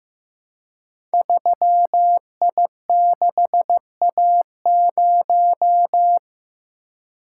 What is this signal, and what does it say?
Morse code, heard as a keyed tone, about 15 wpm, 700 Hz. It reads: 3I6A0
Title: Code Groups - 5 Letter:Numbers - 15wpm